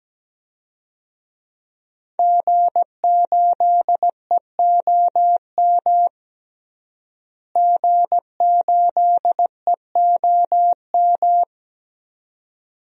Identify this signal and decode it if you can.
Morse code, heard as a keyed tone, about 17 wpm, 700 Hz. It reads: G8EOM G8EOM
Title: DXCC-100 Call Signs - 5 Characters - Repeated 2x - 17wpm